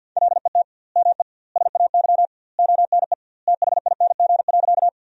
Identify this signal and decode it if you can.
Morse code, heard as a keyed tone, about 50 wpm, 700 Hz. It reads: LET ME SAY ONE THING.
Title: 2x Word-Spacing - Sentences from Top 300 Words - 50wpm